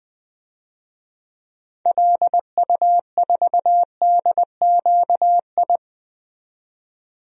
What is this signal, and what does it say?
Morse code, heard as a keyed tone, about 20 wpm, 700 Hz. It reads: LU4DQI
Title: DXCC-100 Call Signs - 6 Characters - 20wpm